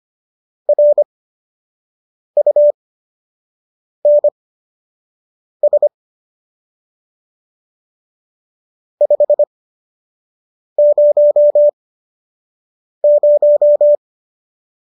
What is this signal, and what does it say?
Morse code, heard as a keyed tone, about 25 wpm, spaced as for 6 wpm, 590 Hz. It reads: RUNS 500